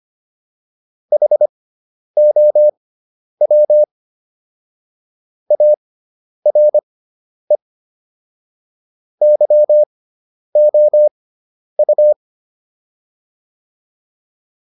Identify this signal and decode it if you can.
Morse code, heard as a keyed tone, about 25 wpm, spaced as for 10 wpm, 600 Hz. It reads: HOW ARE YOU